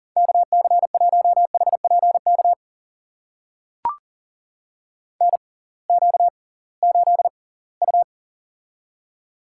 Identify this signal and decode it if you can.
Morse code, heard as a keyed tone, about 40 wpm, spaced as for 23 wpm, 700 Hz. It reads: KC1HPK NQ8U
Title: USA Call Signs - 6x Character Spacing - 40wpm